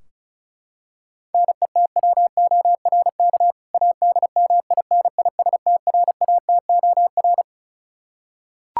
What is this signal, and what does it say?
Morse code, heard as a keyed tone, about 34 wpm, 700 Hz. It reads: NETWORK ADMINISTRATOR
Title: K1USN SST QSO Practice 35wpm